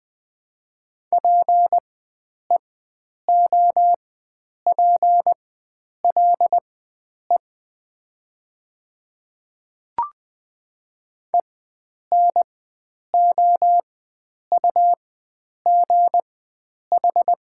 Morse code, heard as a keyed tone, about 20 wpm, spaced as for 9 wpm, 700 Hz. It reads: PEOPLE ENOUGH
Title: 4x Character Spacing - Top 500 Words - 20wpm